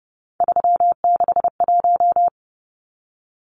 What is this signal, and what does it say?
Morse code, heard as a keyed tone, about 30 wpm, 700 Hz. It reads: 361
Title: Numbers - 3-Digit 30wpm